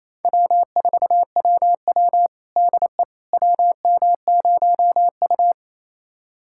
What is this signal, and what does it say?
Morse code, heard as a keyed tone, about 28 wpm, 700 Hz. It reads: W4WW DE WM0U
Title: USA Callsign DE Callsign - 4 Characters - 28wpm